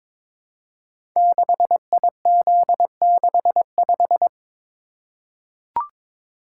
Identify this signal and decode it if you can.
Morse code, heard as a keyed tone, about 22 wpm, 700 Hz. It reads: 6IZ65
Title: Code Groups - 5 Letter:Numbers - 22wpm